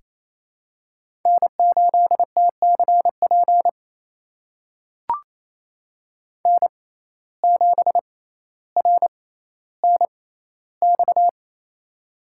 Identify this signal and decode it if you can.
Morse code, heard as a keyed tone, about 28 wpm, spaced as for 14 wpm, 700 Hz. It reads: N8TCP N7RNX
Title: USA Call Signs - 5 Characters - 6x Character Spacing - 28wpm